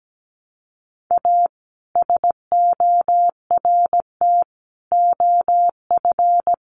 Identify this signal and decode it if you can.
Morse code, heard as a keyed tone, about 17 wpm, 700 Hz. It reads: A SORT OF
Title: Sentences from Top 1300 Words 17wpm